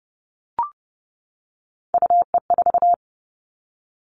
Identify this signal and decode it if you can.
Morse code, heard as a keyed tone, about 30 wpm, 700 Hz. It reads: UE4
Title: Three Challenging Letter-Number Combinations 30wpm